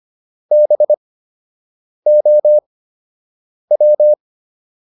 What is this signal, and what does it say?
Morse code, heard as a keyed tone, about 25 wpm, spaced as for 7 wpm, 600 Hz. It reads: BOW